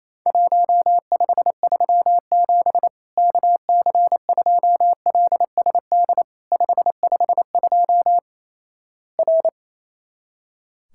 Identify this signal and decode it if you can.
Morse code, heard as a keyed tone, about 28 wpm, 700 Hz. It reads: 1537 KC2LSD 552 R
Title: Worked All Europe DX Contest - QTC Exchange - 28wpm